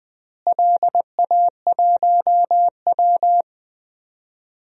Morse code, heard as a keyed tone, about 21 wpm, 700 Hz. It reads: LA1W
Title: DXCC-100 Call Signs - 4 Characters - Repeated 3x - 20wpm